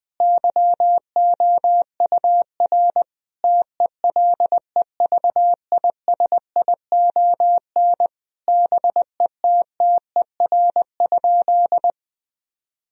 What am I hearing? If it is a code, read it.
Morse code, heard as a keyed tone, about 20 wpm, 700 Hz. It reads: YOUR TELEVISION BETTER?
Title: Sentences from Top 1000 Words - Review Part 4 - 20wpm